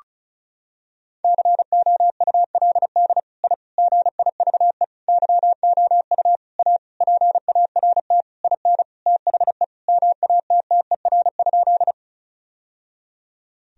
Morse code, heard as a keyed tone, about 34 wpm, 700 Hz. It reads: COULD I GIVE YOU A PART IN THE MATTER?
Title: Sentences from Top 1000 Words - Review Part 1 - 35wpm